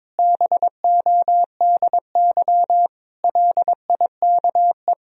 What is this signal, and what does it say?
Morse code, heard as a keyed tone, about 22 wpm, 700 Hz. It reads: BODY LIKE
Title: Sentences from Top 1000 Words - Review Part 2 - 22wpm